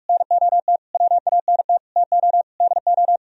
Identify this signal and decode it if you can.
Morse code, heard as a keyed tone, about 49 wpm, 700 Hz. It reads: NOT WANT TO DO